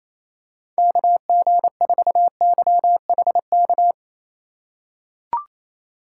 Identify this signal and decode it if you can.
Morse code, heard as a keyed tone, about 28 wpm, 700 Hz. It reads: KG4YHK
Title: USA Call Signs - 2x Character Spacing - 28wpm